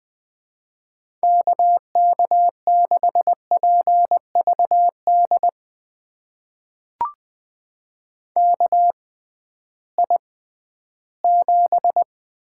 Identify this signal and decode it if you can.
Morse code, heard as a keyed tone, about 20 wpm, spaced as for 13 wpm, 700 Hz. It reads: KK6PVD KI7
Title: USA Call Signs - 6x Character Spacing - 20wpm